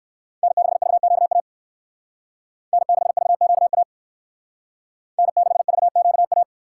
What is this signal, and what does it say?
Morse code, heard as a keyed tone, about 58 wpm, 700 Hz. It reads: N6VQA N6VQA N6VQA